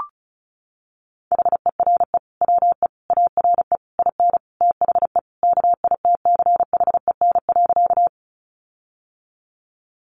Morse code, heard as a keyed tone, about 34 wpm, 700 Hz. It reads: HERE WE ARE IN THE KITCHEN.
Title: Sentences from Top 1400 Words 35wpm